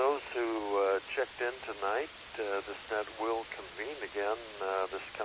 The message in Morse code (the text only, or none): none